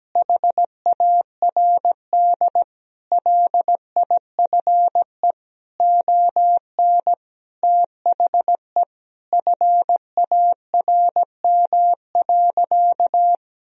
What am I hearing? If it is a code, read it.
Morse code, heard as a keyed tone, about 17 wpm, 700 Hz. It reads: HARD LIFE ON THE FARM.
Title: Sentences from Top 1000 Words - Review Part 2 - 17wpm